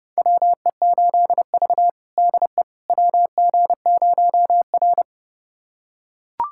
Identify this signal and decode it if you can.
Morse code, heard as a keyed tone, about 30 wpm, 700 Hz. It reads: WE8V DE WG0R
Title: USA Callsign DE Callsign - 4 Characters - 30wpm